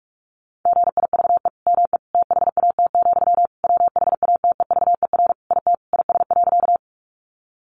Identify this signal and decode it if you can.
Morse code, heard as a keyed tone, about 44 wpm, 700 Hz. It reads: GIVE ME THAT, WHATEVER IT IS.
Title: Sentences from Top 1100 Words 45wpm